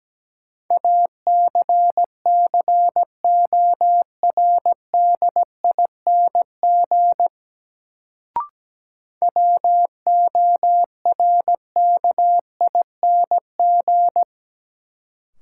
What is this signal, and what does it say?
Morse code, heard as a keyed tone, about 17 wpm, 700 Hz. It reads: ACCORDING WORKING
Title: Sentences from Top 700 Words - New Word Drill - 17wpm